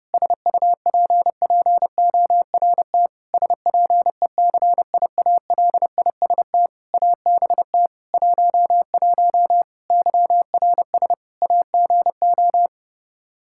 Morse code, heard as a keyed tone, about 32 wpm, 700 Hz. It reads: SUPPORT SPECIALIST ABT 11 YRS AGO
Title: Practice QSO - Exchange 3 - More Info - 30wpm